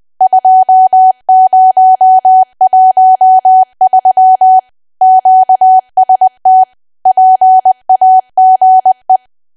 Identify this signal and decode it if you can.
Morse code, heard as a keyed tone, about 20 wpm, 750 Hz. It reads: 2013 QST PAGE
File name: cwa / W1AW-Level3 / 20WPM